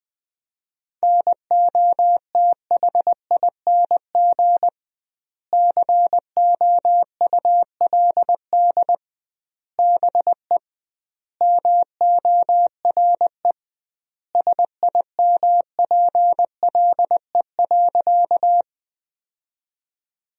Morse code, heard as a keyed tone, about 21 wpm, 700 Hz. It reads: NOTHING COULD BE MORE SIMPLE.
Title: 2x Word-Spacing - Sentences from Top 500 Words - 20wpm